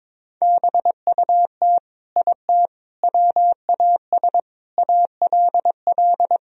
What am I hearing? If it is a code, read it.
Morse code, heard as a keyed tone, about 22 wpm, 700 Hz. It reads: BUT IT WAS ALL
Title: Sentences from Top 1000 Words - Review Part 4 - 22wpm